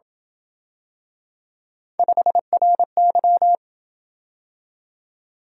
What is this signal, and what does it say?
Morse code, heard as a keyed tone, about 28 wpm, 700 Hz. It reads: 5RY